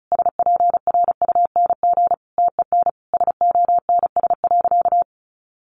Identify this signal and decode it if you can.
Morse code, heard as a keyed tone, about 34 wpm, 700 Hz. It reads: SPRUNG TEN SONS.
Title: A Sound Of Thunder 35wpm